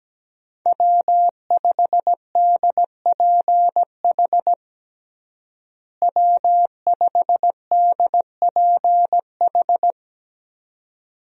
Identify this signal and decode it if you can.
Morse code, heard as a keyed tone, about 17 wpm, 700 Hz. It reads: W5DPH W5DPH
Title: USA Call Signs - 5 Characters - Repeated 2x - 17wpm